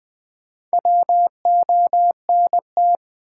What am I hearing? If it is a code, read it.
Morse code, heard as a keyed tone, about 20 wpm, 700 Hz. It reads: WONT
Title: Sentences from Top 1000 Words - Review Part 6 - New Word Drill - 20wpm